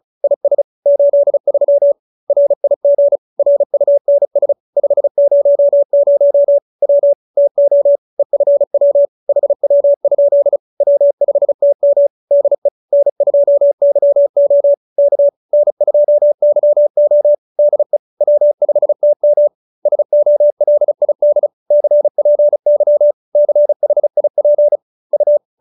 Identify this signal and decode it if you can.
Morse code, heard as a keyed tone, about 35 wpm, 570 Hz. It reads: IS 83 RIG RUNS 500 W TO EFW HW? W5TM DE N2YO K N2YO DE W5TM SOLID CPY CHIP U